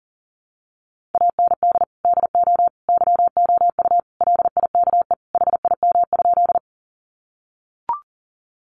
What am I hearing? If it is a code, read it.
Morse code, heard as a keyed tone, about 40 wpm, 700 Hz. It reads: AND DO YOU LIKE HIM?
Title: Sentences from Top 100 Words 40wpm